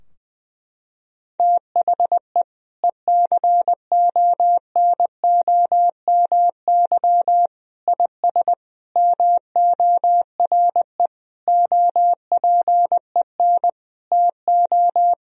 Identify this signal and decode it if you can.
Morse code, heard as a keyed tone, about 20 wpm, 700 Hz. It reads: THE ECONOMY IS MORE OPEN TO
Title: Sentences from Top 1000 Words - Review Part 3 - 20wpm